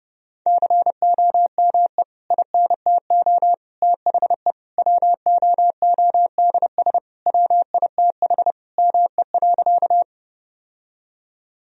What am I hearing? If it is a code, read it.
Morse code, heard as a keyed tone, about 30 wpm, 700 Hz. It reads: COME INTO THE WOODS WITH ME.
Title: Sentences from Top 1000 Words - Review Part 2 - 30wpm